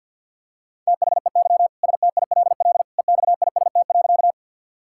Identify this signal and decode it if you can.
Morse code, heard as a keyed tone, about 52 wpm, 690 Hz. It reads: THEY STILL EXIST.
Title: Sentences from Top 1000 Words - Review Part 5 - 50wpm